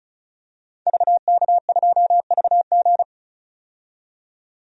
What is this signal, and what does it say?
Morse code, heard as a keyed tone, about 34 wpm, 700 Hz. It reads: VK2VG
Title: DXCC-100 Call Signs - 5 Characters - 35wpm